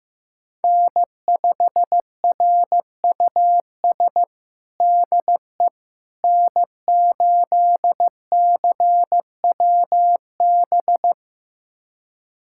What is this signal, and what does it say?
Morse code, heard as a keyed tone, about 15 wpm, 700 Hz. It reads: N5RUS DE N8CWB
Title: USA Callsign DE Callsign - 5 Characters - 15wpm